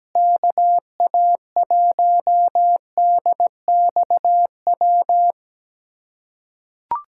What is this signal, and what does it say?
Morse code, heard as a keyed tone, about 17 wpm, 700 Hz. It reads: KA1DXW
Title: USA Callsign DE Callsign - 6 Characters - 17wpm